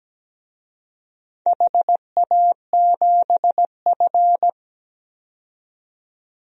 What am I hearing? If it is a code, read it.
Morse code, heard as a keyed tone, about 17 wpm, 700 Hz. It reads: HA7F
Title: DXCC-100 Call Signs - 4 Characters - 17wpm